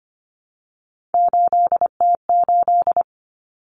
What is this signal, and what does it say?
Morse code, heard as a keyed tone, about 25 wpm, 700 Hz. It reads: 8T8
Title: Common Cut Numbers - 3-Digit 25wpm